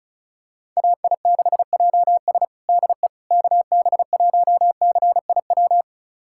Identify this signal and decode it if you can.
Morse code, heard as a keyed tone, about 36 wpm, 700 Hz. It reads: AI6JS DE KB1CIW
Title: USA Callsign DE Callsign 35wpm